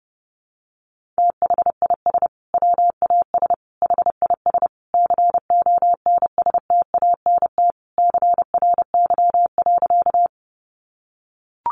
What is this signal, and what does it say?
Morse code, heard as a keyed tone, about 30 wpm, 700 Hz. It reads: THIS WAS HIS CONSTANT CRY.
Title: Sentences from Top 1500 Words 30wpm